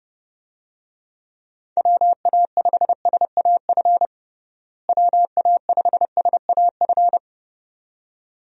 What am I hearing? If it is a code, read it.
Morse code, heard as a keyed tone, about 30 wpm, 700 Hz. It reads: WA5SAF WA5SAF
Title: USA Call Signs - 6 Characters - Repeated 2x - 30wpm